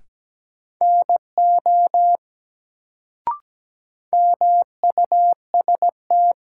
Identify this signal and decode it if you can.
Morse code, heard as a keyed tone, about 17 wpm, 700 Hz. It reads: NO MUST
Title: Sets of 2 Words - New Word Drill - 17wpm